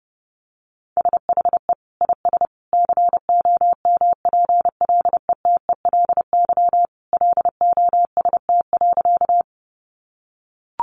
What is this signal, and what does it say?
Morse code, heard as a keyed tone, about 30 wpm, 700 Hz. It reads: SHE IS COMPLETELY LOST.
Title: Additional Easy Phrases 30wpm